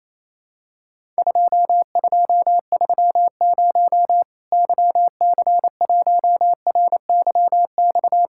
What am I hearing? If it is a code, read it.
Morse code, heard as a keyed tone, about 28 wpm, 700 Hz. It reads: 2230 YC1RYX
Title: Worked All Europe DX Contest - QTC Exchange - 28wpm